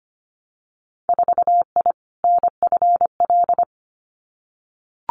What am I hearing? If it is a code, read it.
Morse code, heard as a keyed tone, about 25 wpm, 700 Hz. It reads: 4I NFL
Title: Winter Field Day Exchanges 25wpm